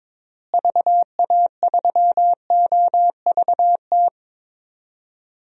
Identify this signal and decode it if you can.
Morse code, heard as a keyed tone, about 22 wpm, 700 Hz. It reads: VA3OVT
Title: DXCC-100 Call Signs - 6 Characters - Repeated 3x - 22wpm